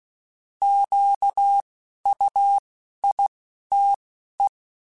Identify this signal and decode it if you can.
Morse code, heard as a keyed tone, about 16 wpm, 800 Hz. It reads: Q U I T E
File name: Misc / Misc-mp3s / Misc @10